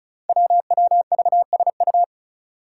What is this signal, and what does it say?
Morse code, heard as a keyed tone, about 36 wpm, 700 Hz. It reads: WWVSU
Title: Five Challenging Letter-Number Combinations 35wpm